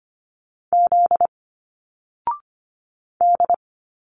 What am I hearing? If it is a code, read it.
Morse code, heard as a keyed tone, about 25 wpm, 700 Hz. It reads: Z D
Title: Single Letters 25wpm